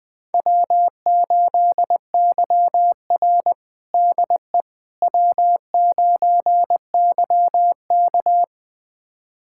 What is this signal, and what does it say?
Morse code, heard as a keyed tone, about 20 wpm, 700 Hz. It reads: W8YR DE W9YK
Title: USA Callsign DE Callsign - 4 Characters - 20wpm